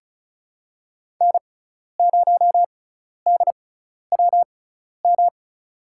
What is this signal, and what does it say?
Morse code, heard as a keyed tone, about 36 wpm, spaced as for 12 wpm, 700 Hz. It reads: N0DWM
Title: USA Call Signs - 5 Characters - 6x Character Spacing - 35wpm